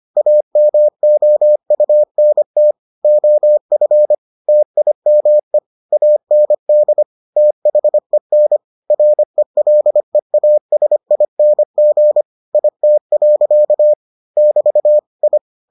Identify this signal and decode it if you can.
Morse code, heard as a keyed tone, about 25 wpm, 600 Hz. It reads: AMOUNT OF TIME AND THEN RELEASING IT. <BT> I